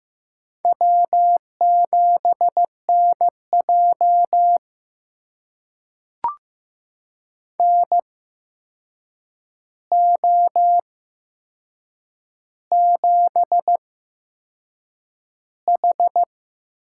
Morse code, heard as a keyed tone, about 15 wpm, spaced as for 6 wpm, 700 Hz. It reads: W7NJ NO7H
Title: USA Call Signs - 4 Characters - 8x Character Spacing - 15wpm